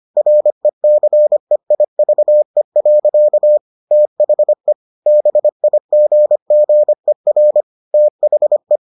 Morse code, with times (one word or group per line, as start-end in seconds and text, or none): 0.2-3.6 RECEIVE.
3.9-4.7 THE
5.1-7.6 BIGGER
7.9-8.8 THE